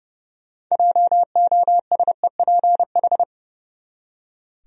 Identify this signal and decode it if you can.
Morse code, heard as a keyed tone, about 29 wpm, 700 Hz. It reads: JOSEPH
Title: USA Common Names - 30wpm